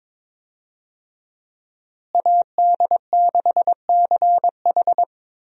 Morse code, heard as a keyed tone, about 22 wpm, 700 Hz. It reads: AD6CH
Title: USA Call Signs - 5 Characters - Repeated 2x - 22wpm